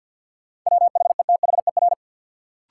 Morse code, heard as a keyed tone, about 53 wpm, 700 Hz. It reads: WHETHER
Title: Sets of 3 Words - New Word Drill - 50wpm